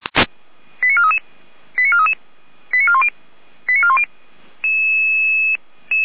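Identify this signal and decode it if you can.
Five-tone selective call (ZVEI), heard as quick runs of five tones, tones of about 70 ms. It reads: 89433 89433 89410 89410
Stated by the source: ZVEI five-tone call 88411, received on SDR.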